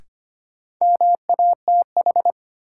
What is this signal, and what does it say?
Morse code, heard as a keyed tone, about 25 wpm, 700 Hz. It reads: MATH
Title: Sentences from Top 1000 Words - Review Part 6 - New Word Drill - 25wpm